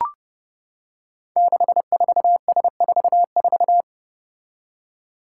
Morse code, heard as a keyed tone, about 30 wpm, 700 Hz. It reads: 64S44
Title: Five Challenging Letter-Number Combinations 30wpm